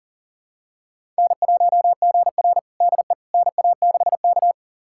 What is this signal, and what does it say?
Morse code, heard as a keyed tone, about 40 wpm, 700 Hz. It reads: N1GR DE NA6K
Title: USA Callsign DE Callsign - 4 Characters - 40wpm